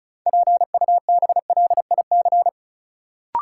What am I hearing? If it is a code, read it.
Morse code, heard as a keyed tone, about 34 wpm, 700 Hz. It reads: PUBLIC
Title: Sets of 2 Words - New Word Drill - 35wpm